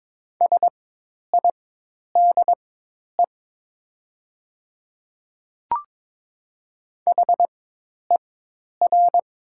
Morse code, heard as a keyed tone, about 22 wpm, spaced as for 10 wpm, 700 Hz. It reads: SIDE HER
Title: 4x Character Spacing - Top 100 Words - 22wpm